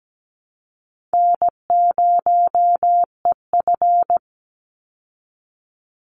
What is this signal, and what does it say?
Morse code, heard as a keyed tone, about 17 wpm, 700 Hz. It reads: N0EF
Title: Call Signs Repeated 2x - 17wpm